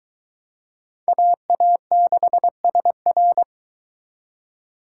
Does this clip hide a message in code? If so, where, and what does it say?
Yes: Morse code from 1.1 to 3.4 s, heard as a keyed tone, about 23 wpm, 700 Hz. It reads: AA6SR